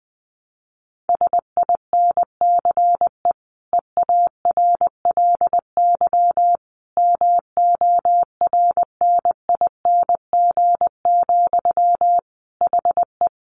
Morse code, heard as a keyed tone, about 20 wpm, 700 Hz. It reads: SINCE EARLY MORNING, HE
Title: Sentences from Top 1400 Words 20wpm